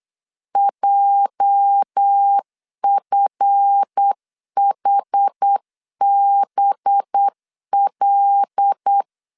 Morse code, heard as a keyed tone, about 8 wpm, 800 Hz. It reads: JFHBL
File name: LEZIONI / 8 YJ45 / 8 YJ4 8wpm 3246